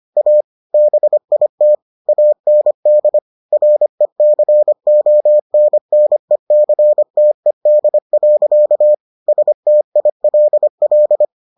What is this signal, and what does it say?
Morse code, heard as a keyed tone, about 25 wpm, 600 Hz. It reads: A BIT AND RECONNECTED. STILL